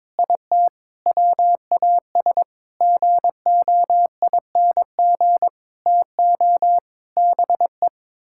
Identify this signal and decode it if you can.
Morse code, heard as a keyed tone, about 22 wpm, 700 Hz. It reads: IT WAS GOING TO BE
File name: practice / 22wpm / Sets of 7 Words - 22wpm